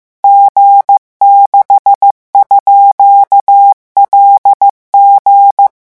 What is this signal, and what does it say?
Morse code, heard as a keyed tone, about 15 wpm, 800 Hz. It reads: G6_LG
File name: morse mp3 / revision12-vitesse 14 mots